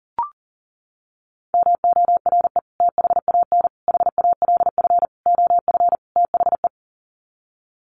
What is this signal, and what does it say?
Morse code, heard as a keyed tone, about 40 wpm, 700 Hz. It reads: MORE THAN HALF OF THE
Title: Sets of 5 Words 40wpm